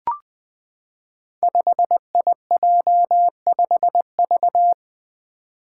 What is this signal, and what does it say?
Morse code, heard as a keyed tone, about 20 wpm, 700 Hz. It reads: 5IJ5V
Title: Five Challenging Letter-Number Combinations 20wpm